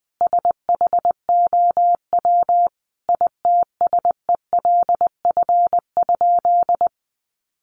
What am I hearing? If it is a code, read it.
Morse code, heard as a keyed tone, about 20 wpm, 700 Hz. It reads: SHOW ITSELF?
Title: Sentences from Top 800 Words 20wpm